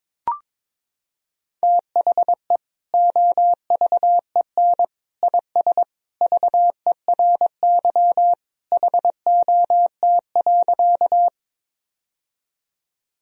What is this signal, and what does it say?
Morse code, heard as a keyed tone, about 22 wpm, 700 Hz. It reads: THE OVEN IS VERY HOT.
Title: Sentences from Top 1000 Words - Review Part 4 - 22wpm